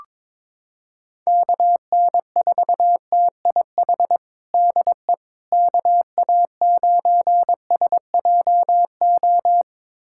Morse code, heard as a keyed tone, about 22 wpm, 700 Hz. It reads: KN4TIH DE KA9SJO
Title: USA Callsign DE Callsign - 6 Characters - 22wpm